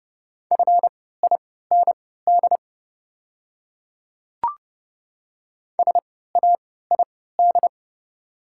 Morse code, heard as a keyed tone, about 29 wpm, spaced as for 17 wpm, 700 Hz. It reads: FIND SAID
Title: 3x Character Spacing - Top 100 Words - 30wpm